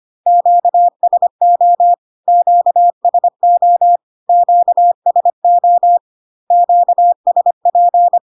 Morse code, heard as a keyed tone, about 25 wpm, 700 Hz. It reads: QSO QSO QSO QSP
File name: Q-codes most common repeated 3 times 25 25wpm